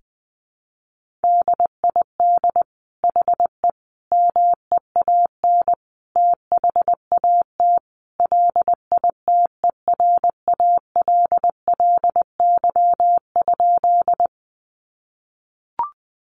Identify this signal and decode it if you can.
Morse code, heard as a keyed tone, about 20 wpm, 700 Hz. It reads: DID HE MEAN THAT LITERALLY?
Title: Sentences from Top 1400 Words 20wpm